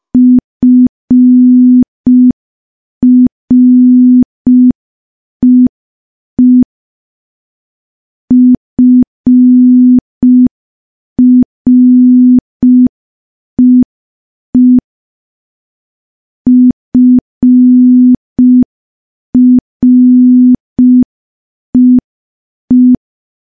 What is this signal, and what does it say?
Morse code, heard as a keyed tone, about 5 wpm, 250 Hz. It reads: FREE FREE FREE
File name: morse / morse-training-05WPM